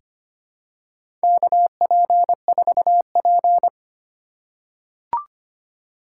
Morse code, heard as a keyed tone, about 25 wpm, 700 Hz. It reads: KP4P
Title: USA Call Signs - 2x Character Spacing - 25wpm